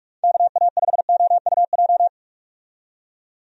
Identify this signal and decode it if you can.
Morse code, heard as a keyed tone, about 46 wpm, 700 Hz. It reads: KA5OUJ
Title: USA Call Signs - 6 Characters - 45wpm